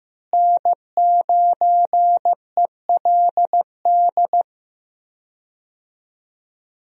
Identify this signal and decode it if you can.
Morse code, heard as a keyed tone, about 15 wpm, 700 Hz. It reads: N9ELD
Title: USA Call Signs - 5 Characters - Repeated 2x - 15wpm